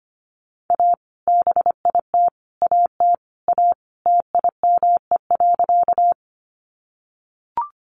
Morse code, heard as a keyed tone, about 25 wpm, 700 Hz. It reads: A BIT AT A TIME.
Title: Sentences from Top 900 Words 25wpm